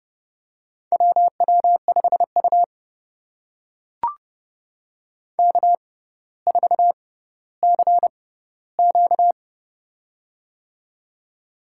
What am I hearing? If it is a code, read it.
Morse code, heard as a keyed tone, about 30 wpm, spaced as for 15 wpm, 700 Hz. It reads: WW5U K4CQ
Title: USA Call Signs - 4 Characters - 6x Character Spacing - 30wpm